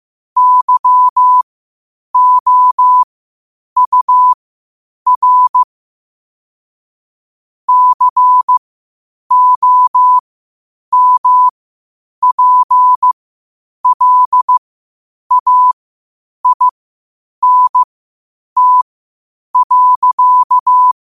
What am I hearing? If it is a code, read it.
Morse code, heard as a keyed tone, about 15 wpm, spaced as for 8 wpm, 1000 Hz. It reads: YOUR COMPLAINT.
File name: qotdfiles / 8wpm